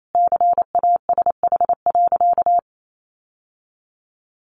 Morse code, heard as a keyed tone, about 28 wpm, 700 Hz. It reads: CASH.
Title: Sentences from Top 1300 Words 28wpm